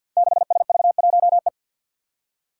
Morse code, heard as a keyed tone, about 52 wpm, 700 Hz. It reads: 6SV1E
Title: Five Challenging Letter-Number Combinations 50wpm